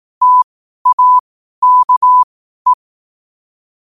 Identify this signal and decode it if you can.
Morse code, heard as a keyed tone, about 18 wpm, spaced as for 12 wpm, 1000 Hz. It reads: TAKE
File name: qotdfiles / 12wpm